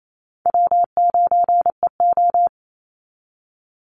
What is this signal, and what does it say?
Morse code, heard as a keyed tone, about 28 wpm, 700 Hz. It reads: W9EO
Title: Easy Call Signs Repeated 2x - 28wpm